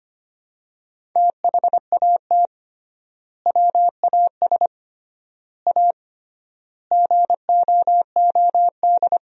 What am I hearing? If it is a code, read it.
Morse code, heard as a keyed tone, about 25 wpm, 700 Hz. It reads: THAT WAS A GOOD
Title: 3x Word-Spacing - Sentences from Top 200 Words - 25wpm